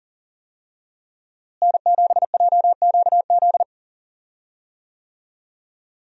Morse code, heard as a keyed tone, about 40 wpm, 700 Hz. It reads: N7JQZ
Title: USA Call Signs - 5 Characters - Repeated 3x - 40wpm